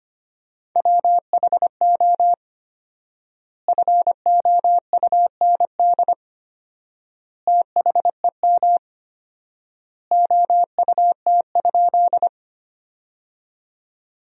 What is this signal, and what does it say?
Morse code, heard as a keyed tone, about 25 wpm, 700 Hz. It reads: WHO FOUND THEM OUT?
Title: 4x Word-Spacing - Sentences from Top 300 Words - 25wpm